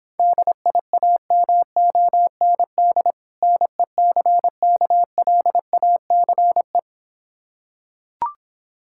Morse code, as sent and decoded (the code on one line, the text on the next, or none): -.. .. .- -- --- -. -.. / -. -.-. -.- .-.. .- -.-. .
DIAMOND NECKLACE